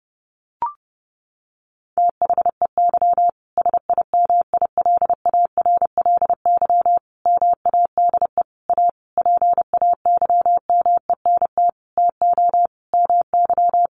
Morse code, heard as a keyed tone, about 30 wpm, 700 Hz. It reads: THEY SIMILARLY MADE A PAYMENT TO MY
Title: Sentences from Top 1400 Words 30wpm